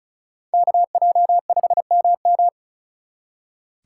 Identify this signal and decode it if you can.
Morse code, heard as a keyed tone, about 39 wpm, 700 Hz. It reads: KJ5MM
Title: Speed Racing - USA 2x2 Call Signs 35wpm